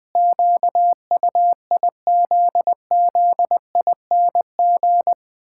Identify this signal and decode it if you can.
Morse code, heard as a keyed tone, about 20 wpm, 700 Hz. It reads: QUIZZING
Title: Pangram Challenge v2 - 20wpm